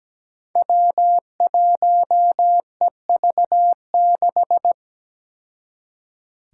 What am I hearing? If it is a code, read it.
Morse code, heard as a keyed tone, about 17 wpm, 700 Hz. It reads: W1EV6
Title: Five Challenging Letter-Number Combinations 17wpm